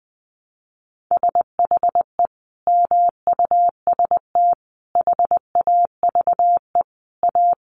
Morse code, heard as a keyed tone, about 20 wpm, 700 Hz. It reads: SHE MUST HAVE A
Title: Sentences from Top 300 Words 20wpm